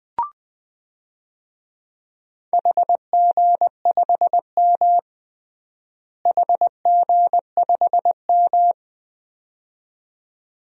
Morse code, heard as a keyed tone, about 21 wpm, 700 Hz. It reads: HG5M HG5M
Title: DXCC-100 Call Signs - 4 Characters - Repeated 2x - 20wpm